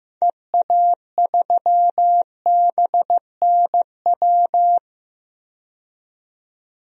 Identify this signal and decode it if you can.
Morse code, heard as a keyed tone, about 15 wpm, 700 Hz. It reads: EA3BNW
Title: DXCC-100 Call Signs - 6 Characters - 15wpm